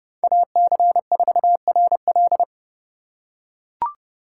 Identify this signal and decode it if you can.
Morse code, heard as a keyed tone, about 28 wpm, 700 Hz. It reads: AC4RL